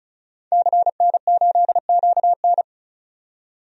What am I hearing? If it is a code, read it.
Morse code, heard as a keyed tone, about 36 wpm, 700 Hz. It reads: CN8QN
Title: DXCC-100 Call Signs - 5 Characters - 35wpm